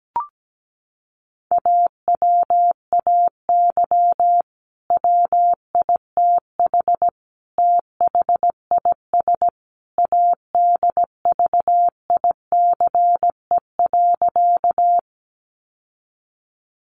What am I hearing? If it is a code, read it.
Morse code, heard as a keyed tone, about 17 wpm, 700 Hz. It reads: AWAY WITH THIS ADVICE.
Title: Sentences from Top 1100 Words 17wpm